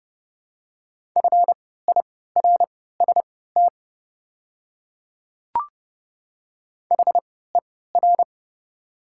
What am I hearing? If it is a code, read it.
Morse code, heard as a keyed tone, about 30 wpm, spaced as for 17 wpm, 700 Hz. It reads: FIRST HER
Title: 3x Character Spacing - Top 200 Words - 30wpm